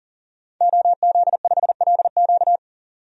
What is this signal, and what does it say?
Morse code, heard as a keyed tone, about 40 wpm, 700 Hz. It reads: OZ5LQ
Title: DXCC-100 Call Signs - 5 Characters - 40wpm